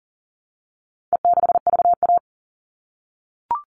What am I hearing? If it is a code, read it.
Morse code, heard as a keyed tone, about 40 wpm, 700 Hz. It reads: E6VA
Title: All Cut Numbers - 4-Digit 40wpm